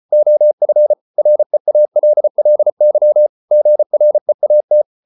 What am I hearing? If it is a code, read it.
Morse code, heard as a keyed tone, about 34 wpm, 600 Hz. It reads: OF REALLY GREAT